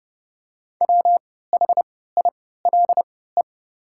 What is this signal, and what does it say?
Morse code, heard as a keyed tone, about 30 wpm, spaced as for 17 wpm, 690 Hz. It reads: WHILE
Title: 3x Character Spacing - Top 300 Words - 30wpm